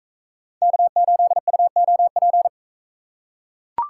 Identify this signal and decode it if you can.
Morse code, heard as a keyed tone, about 41 wpm, 700 Hz. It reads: K8UOP